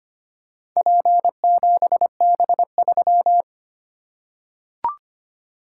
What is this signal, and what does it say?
Morse code, heard as a keyed tone, about 25 wpm, 700 Hz. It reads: P7B3
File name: practice / 25wpm / Four Letter-Number Combinations - 25wpm